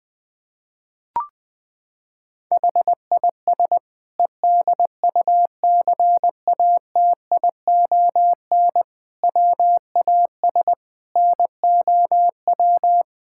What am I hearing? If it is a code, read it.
Morse code, heard as a keyed tone, about 21 wpm, 700 Hz. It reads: HIS EDUCATION WAS NOW
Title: Sentences from Top 1000 Words - Review Part 6 - 20wpm